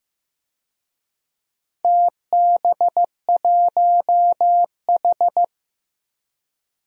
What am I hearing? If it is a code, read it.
Morse code, heard as a keyed tone, about 15 wpm, 700 Hz. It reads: TB1H
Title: DXCC-100 Call Signs - 4 Characters - Repeated 3x - 15wpm